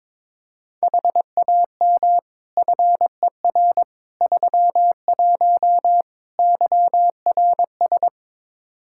Morse code, heard as a keyed tone, about 22 wpm, 700 Hz. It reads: HAM FER 31 YRS
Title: Practice QSO - Exchange 3 - More Info - 22wpm